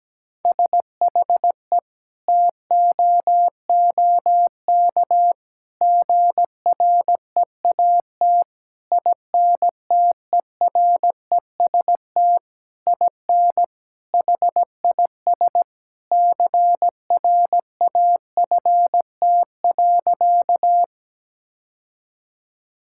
Morse code, heard as a keyed tone, about 17 wpm, 700 Hz. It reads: SHE TOOK GREAT INTEREST IN HIS CRAFT.
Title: Sentences from Top 1000 Words - Review Part 6 - 17wpm